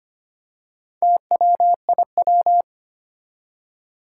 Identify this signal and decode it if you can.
Morse code, heard as a keyed tone, about 25 wpm, 700 Hz. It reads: TWIW
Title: Four Challenging Letter-Number Combinations 25wpm